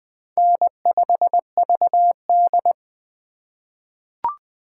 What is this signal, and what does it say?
Morse code, heard as a keyed tone, about 21 wpm, 700 Hz. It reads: N5VD
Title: USA Call Signs - 2x Character Spacing - 20wpm